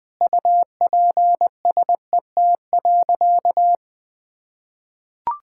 Morse code, heard as a keyed tone, about 21 wpm, 700 Hz. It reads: UPSET.
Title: Sentences from Top 1000 Words - Review Part 5 - 20wpm